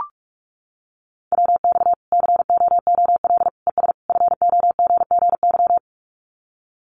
Morse code, heard as a keyed tone, about 44 wpm, 700 Hz. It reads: WX COOL ES FOGGY
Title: Common QSO Phrases 45wpm